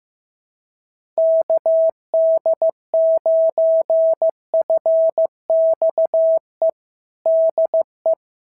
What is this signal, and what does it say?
Morse code, heard as a keyed tone, about 15 wpm, 650 Hz. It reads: KD9FXE DE
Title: SOTA QSO Practice 15wpm